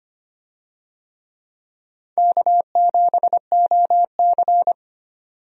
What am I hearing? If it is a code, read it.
Morse code, heard as a keyed tone, about 25 wpm, 700 Hz. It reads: K7OC
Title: USA Call Signs - 4 Characters - 25wpm